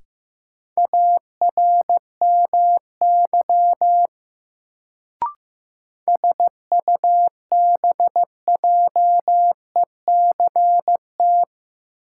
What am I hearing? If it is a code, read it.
Morse code, heard as a keyed tone, about 15 wpm, 700 Hz. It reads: ARMY SUBJECT
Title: Sentences from Top 800 Words - New Word Drill - 15wpm